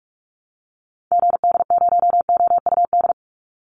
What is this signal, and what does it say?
Morse code, heard as a keyed tone, about 44 wpm, 700 Hz. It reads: GD0OUD
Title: Call Signs Repeated 2x - 45wpm